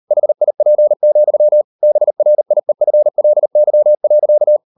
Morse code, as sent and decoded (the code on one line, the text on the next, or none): .... .. .--. --..-- / -... .-. .. ..-. .-.. -.-- .-.-.-
HIP, BRIEFLY.